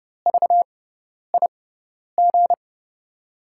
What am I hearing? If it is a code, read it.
Morse code, heard as a keyed tone, about 30 wpm, spaced as for 10 wpm, 700 Hz. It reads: VIG
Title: USA Call Signs - 5 Characters - 6x Character Spacing - 30wpm